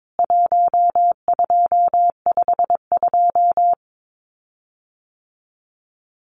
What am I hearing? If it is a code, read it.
Morse code, heard as a keyed tone, about 22 wpm, 700 Hz. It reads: 1252
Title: CWT Exchange 22wpm